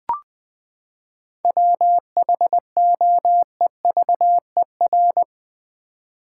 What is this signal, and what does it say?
Morse code, heard as a keyed tone, about 20 wpm, 700 Hz. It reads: WHOEVER
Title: Seven Letter Words - 20wpm